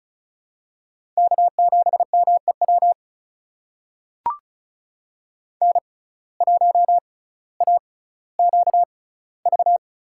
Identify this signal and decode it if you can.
Morse code, heard as a keyed tone, about 35 wpm, spaced as for 18 wpm, 700 Hz. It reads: K7MEW N1AQV
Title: USA Call Signs - 5 Characters - 6x Character Spacing - 35wpm